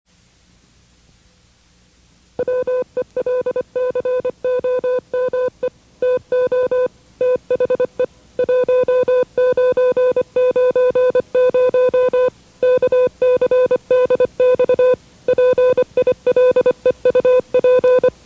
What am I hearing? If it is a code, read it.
Morse code, heard as a keyed tone, about 25 wpm, 500 Hz. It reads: WELCOME TO THE 1990 KCDX PILEUP
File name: WRTC1990-CW-Pileup